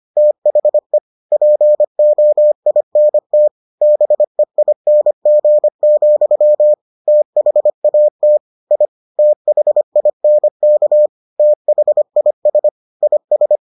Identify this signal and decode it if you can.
Morse code, heard as a keyed tone, about 25 wpm, 600 Hz. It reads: THE POINT BEING, THAT I THINK THIS IS